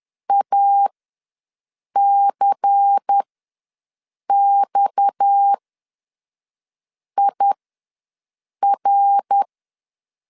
Morse code, heard as a keyed tone, about 10 wpm, spaced as for 5 wpm, 800 Hz. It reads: ACXIR
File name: LEZIONI / 6 PKX0 / 6 PKX0 ws 10wpm 4220